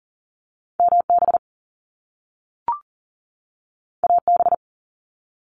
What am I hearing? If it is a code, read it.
Morse code, heard as a keyed tone, about 40 wpm, 700 Hz. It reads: MB AB
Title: Canadian Provinces and Territories 40wpm